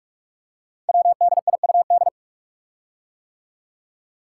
Morse code, heard as a keyed tone, about 44 wpm, 700 Hz. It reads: WDIUD
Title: Code Groups - 5 Letter:Numbers - 45wpm